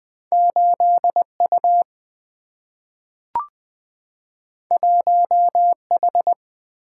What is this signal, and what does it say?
Morse code, heard as a keyed tone, about 20 wpm, 700 Hz. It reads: 8 U 1 H
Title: Single-Letter Single-Number 20wpm